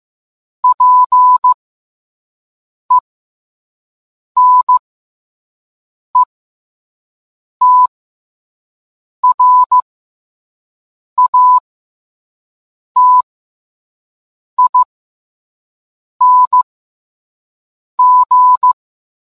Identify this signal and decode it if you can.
Morse code, heard as a keyed tone, about 15 wpm, spaced as for 5 wpm, 1000 Hz. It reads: PENETRATING